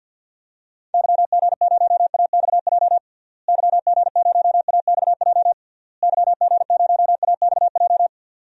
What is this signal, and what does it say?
Morse code, heard as a keyed tone, about 51 wpm, 700 Hz. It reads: YG0AXJ YG0AXJ YG0AXJ
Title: DXCC-100 Call Signs - 6 Characters - Repeated 3x - 50wpm